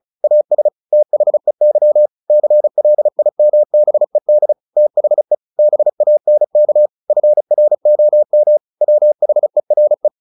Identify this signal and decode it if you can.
Morse code, heard as a keyed tone, about 35 wpm, 600 Hz. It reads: AS THEY CLIMBED THE BANK FROM WHERE
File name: the old man and the sea / the old man and the sea 1 35wpm